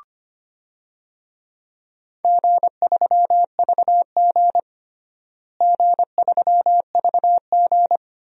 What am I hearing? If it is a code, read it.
Morse code, heard as a keyed tone, about 25 wpm, 700 Hz. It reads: G3VG G3VG
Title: DXCC-100 Call Signs - 4 Characters - Repeated 2x - 25wpm